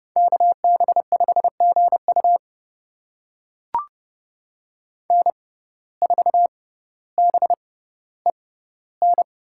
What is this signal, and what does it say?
Morse code, heard as a keyed tone, about 30 wpm, spaced as for 15 wpm, 700 Hz. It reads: KB5GU N4BEN
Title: USA Call Signs - 6x Character Spacing - 30wpm